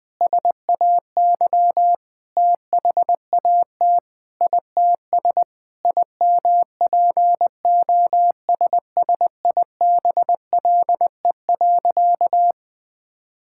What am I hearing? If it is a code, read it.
Morse code, heard as a keyed tone, about 19 wpm, 700 Hz. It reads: SAY THAT ITS IMPOSSIBLE.
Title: Sentences with 6-Word Phrases - Spoken Beginning - 20wpm